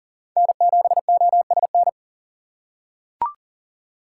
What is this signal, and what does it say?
Morse code, heard as a keyed tone, about 37 wpm, 690 Hz. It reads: N7OSN
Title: USA Call Signs - 5 Characters - 6x Character Spacing - 40wpm